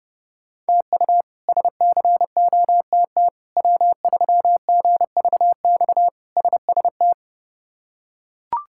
Morse code, heard as a keyed tone, about 29 wpm, 700 Hz. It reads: TU SCOTT W3GVX SST
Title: K1USN SST QSO Practice - 30wpm